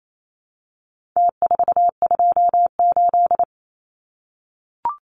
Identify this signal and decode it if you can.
Morse code, heard as a keyed tone, about 28 wpm, 700 Hz. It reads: T428
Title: Numbers - Contest Style - 4-Digit 28wpm